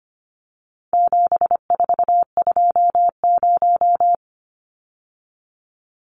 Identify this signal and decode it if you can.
Morse code, heard as a keyed tone, about 25 wpm, 700 Hz. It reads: 7420
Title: Numbers - 4-Digit 25wpm